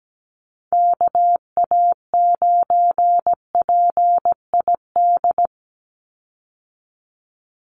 Morse code, heard as a keyed tone, about 17 wpm, 700 Hz. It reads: KA9PID
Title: USA Call Signs 17wpm